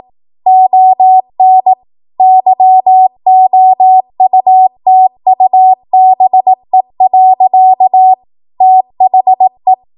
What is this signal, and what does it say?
Morse code, heard as a keyed tone, about 18 wpm, 750 Hz. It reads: ON YOUTUBE. THE
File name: Morse / 18WPM